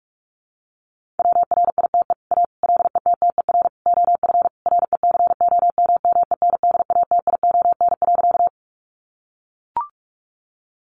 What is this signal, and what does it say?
Morse code, heard as a keyed tone, about 44 wpm, 700 Hz. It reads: WRITE A LETTER OF RECOMMENDATION.
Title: Sentences from Top 1400 Words 45wpm